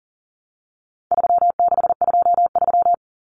Morse code, heard as a keyed tone, about 40 wpm, 700 Hz. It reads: 3623
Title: Numbers - 4-Digit 40wpm